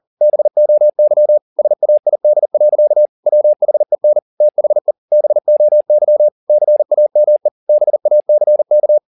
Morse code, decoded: BOY SAID. WHEN THE BOY CAME BACK

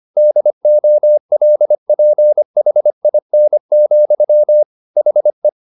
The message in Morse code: DOLPHIN, HE

